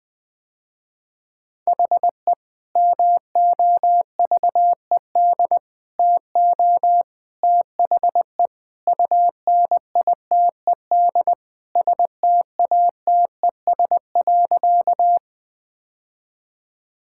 Morse code, heard as a keyed tone, about 20 wpm, 700 Hz. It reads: HE MOVED TO THE UNITED STATES.
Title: Sentences with 3-Word Phrases - Spoken Beginning - 20wpm